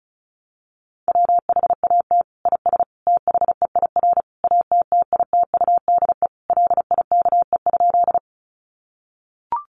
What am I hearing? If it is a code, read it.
Morse code, heard as a keyed tone, about 34 wpm, 700 Hz. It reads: WHAT IS THEIR ATTITUDE LIKE?
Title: Sentences from Top 1300 Words 35wpm